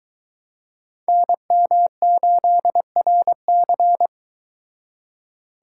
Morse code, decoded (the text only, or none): NM8RC